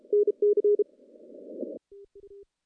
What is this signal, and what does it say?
Morse code, heard as a keyed tone, about 34 wpm, 410 Hz. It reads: NC
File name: CW-SS-k5tr